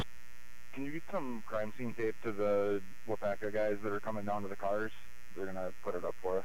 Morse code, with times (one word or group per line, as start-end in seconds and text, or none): none